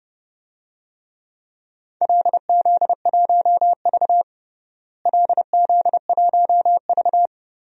Morse code, heard as a keyed tone, about 30 wpm, 700 Hz. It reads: LZ1V LZ1V
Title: DXCC-100 Call Signs - 4 Characters - Repeated 2x - 30wpm